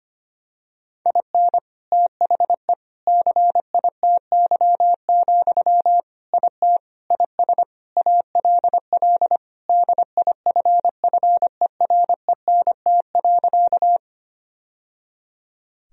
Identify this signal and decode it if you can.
Morse code, heard as a keyed tone, about 25 wpm, 700 Hz. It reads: IN THE CITY, IT IS ALL DIFFERENT.
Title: Sentences from Top 1000 Words - Review Part 3 - 25wpm